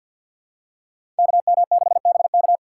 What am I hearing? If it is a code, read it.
Morse code, heard as a keyed tone, about 46 wpm, 690 Hz. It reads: KM6BK
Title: USA Call Signs - 3x Character Spacing - 50wpm